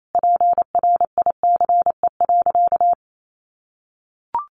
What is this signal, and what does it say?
Morse code, heard as a keyed tone, about 28 wpm, 700 Hz. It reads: PRICE.
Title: Sentences from Top 1200 Words 28wpm